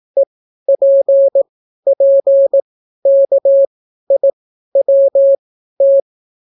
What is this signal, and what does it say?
Morse code, heard as a keyed tone, about 18 wpm, spaced as for 12 wpm, 550 Hz. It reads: EPPKIWT